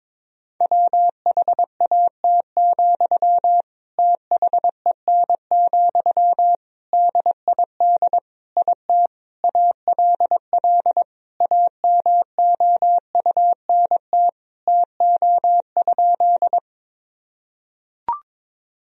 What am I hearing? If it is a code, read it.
Morse code, heard as a keyed tone, about 22 wpm, 700 Hz. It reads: WHAT, THEN, DID IT ALL AMOUNT TO?
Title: Sentences from Top 1000 Words - Review Part 4 - 22wpm